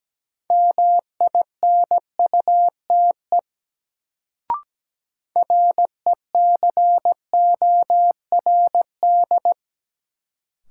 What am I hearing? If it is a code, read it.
Morse code, heard as a keyed tone, about 17 wpm, 700 Hz. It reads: MINUTE RECORD
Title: Sentences from Top 500 Words - New Word Drill - 17wpm